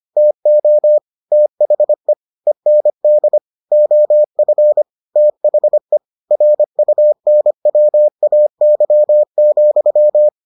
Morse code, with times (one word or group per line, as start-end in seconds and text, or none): 0.2-1.0 TO
1.3-2.1 THE
2.5-3.4 END
3.7-4.8 OF
5.2-6.0 THE
6.3-10.3 RUNWAY,